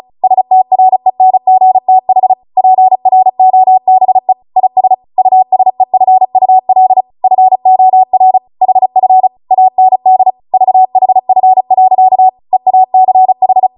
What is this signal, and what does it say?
Morse code, heard as a keyed tone, about 36 wpm, 750 Hz. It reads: STRENGTH PROBE IS USEFUL FOR HF AND VHF. EACH